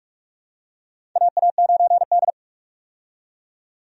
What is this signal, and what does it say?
Morse code, heard as a keyed tone, about 45 wpm, 700 Hz. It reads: AA9D
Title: USA Call Signs - 4 Characters - 45wpm